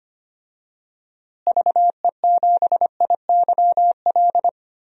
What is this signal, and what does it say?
Morse code, heard as a keyed tone, about 25 wpm, 700 Hz. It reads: VE7IYL
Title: DXCC-100 Call Signs - 6 Characters - Repeated 3x - 25wpm